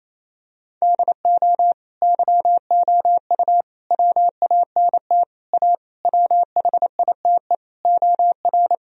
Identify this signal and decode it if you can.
Morse code, heard as a keyed tone, about 28 wpm, 700 Hz. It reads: DO YOU WANT A WHITE OR
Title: Sentences from Top 1000 Words - Review Part 2 - 28wpm